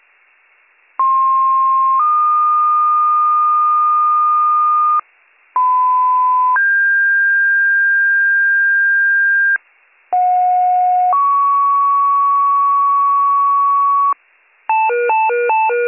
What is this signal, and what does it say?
Two-tone page, heard as a one-second tone, then a three-second tone, about 1060 then 1220 Hz, about 1010 then 1600 Hz, about 730 then 1120 Hz. It is heard from 1.0 to 14.1 s.